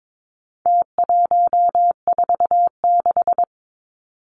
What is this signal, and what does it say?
Morse code, heard as a keyed tone, about 22 wpm, 700 Hz. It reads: T146
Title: Numbers - Contest Style - 4-Digit 22wpm